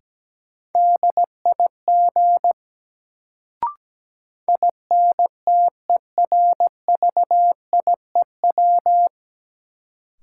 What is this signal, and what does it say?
Morse code, heard as a keyed tone, about 17 wpm, 700 Hz. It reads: DIG INTERVIEW
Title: Sentences from Top 1600 Words - New Word Drill Alt - 17wpm